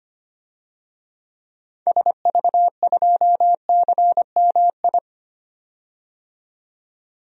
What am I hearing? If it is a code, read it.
Morse code, heard as a keyed tone, about 25 wpm, 700 Hz. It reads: SV2CMI
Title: DXCC-100 Call Signs - 6 Characters - 25wpm